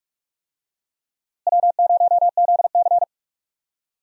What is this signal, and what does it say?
Morse code, heard as a keyed tone, about 46 wpm, 700 Hz. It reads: W0ZC
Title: USA Call Signs - 4 Characters - Repeated 3x - 45wpm